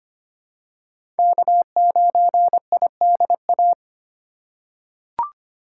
Morse code, heard as a keyed tone, about 25 wpm, 700 Hz. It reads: K9IDA
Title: USA Call Signs - 4x Character Spacing - 25wpm